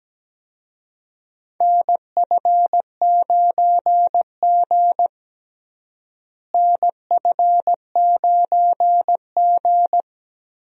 Morse code, heard as a keyed tone, about 17 wpm, 700 Hz. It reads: NF9G NF9G
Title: USA Call Signs - 4 Characters - Repeated 3x - 17wpm